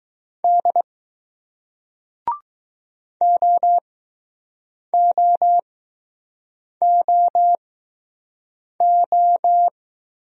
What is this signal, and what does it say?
Morse code, heard as a keyed tone, about 20 wpm, spaced as for 6 wpm, 700 Hz. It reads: D OOOO